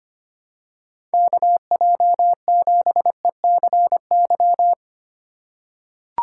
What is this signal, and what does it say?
Morse code, heard as a keyed tone, about 25 wpm, 700 Hz. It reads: KJ7ECY
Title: USA Call Signs - 6 Characters - 3x Character Spacing - 25wpm